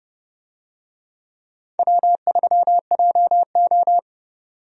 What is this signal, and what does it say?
Morse code, heard as a keyed tone, about 32 wpm, 700 Hz. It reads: W3JO
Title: USA Call Signs - 4 Characters - 30wpm